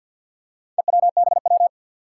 Morse code, heard as a keyed tone, about 52 wpm, 700 Hz. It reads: EWBW
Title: Four Challenging Letter-Number Combinations 50wpm